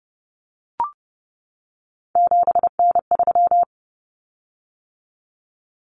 Morse code, heard as a keyed tone, about 30 wpm, 700 Hz. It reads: 7N3
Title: Common Cut Numbers - 3-Digit 30wpm